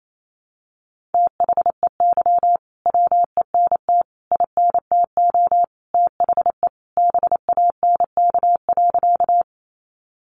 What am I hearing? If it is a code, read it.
Morse code, heard as a keyed tone, about 28 wpm, 700 Hz. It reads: THEY WENT INTO THE BANK.
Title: Sentences from Top 900 Words 28wpm